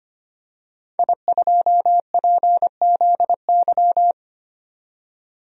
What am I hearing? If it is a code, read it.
Morse code, heard as a keyed tone, about 25 wpm, 700 Hz. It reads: I2PZY
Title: DXCC-100 Call Signs - 5 Characters - 25wpm